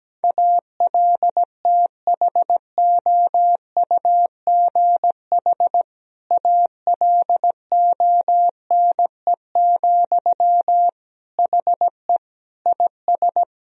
Morse code, heard as a keyed tone, about 17 wpm, 700 Hz. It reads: ALTHOUGH ALONE, HE IS
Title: Sentences from Top 1000 Words - Review Part 3 - 17wpm